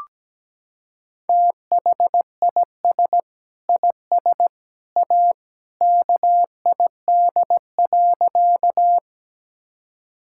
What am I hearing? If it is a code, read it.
Morse code, heard as a keyed tone, about 17 wpm, 700 Hz. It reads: THIS IS A KID.
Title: Sentences from Top 1600 Words 17wpm